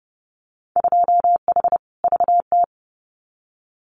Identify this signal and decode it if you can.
Morse code, heard as a keyed tone, about 30 wpm, 700 Hz. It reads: 2H VT
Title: Winter Field Day Exchanges 30wpm